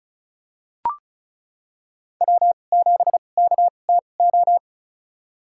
Morse code, heard as a keyed tone, about 36 wpm, spaced as for 25 wpm, 700 Hz. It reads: W7KTO
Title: USA Call Signs - 5 Characters - 2x Character Spacing - 35wpm